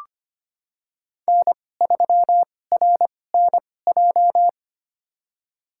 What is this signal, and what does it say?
Morse code, heard as a keyed tone, about 25 wpm, spaced as for 18 wpm, 700 Hz. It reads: N3RNJ
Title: USA Call Signs - 5 Characters - 2x Character Spacing - 25wpm